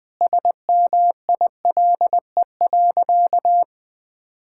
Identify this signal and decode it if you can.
Morse code, heard as a keyed tone, about 21 wpm, 700 Hz. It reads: SMILE.
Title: Sentences with 2-Word Phrases - 20wpm